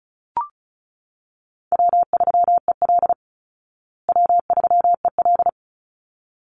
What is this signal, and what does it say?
Morse code, heard as a keyed tone, about 34 wpm, 700 Hz. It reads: W3EL W3EL
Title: Easy Call Signs Repeated 2x - 35wpm